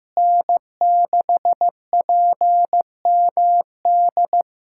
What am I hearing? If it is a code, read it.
Morse code, heard as a keyed tone, about 15 wpm, 700 Hz. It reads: N6PMD
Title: USA Callsign DE Callsign - 5 Characters - 15wpm